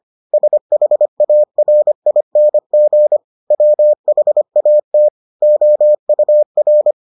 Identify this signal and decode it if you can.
Morse code, heard as a keyed tone, about 25 wpm, 600 Hz. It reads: SHARING WHAT OUR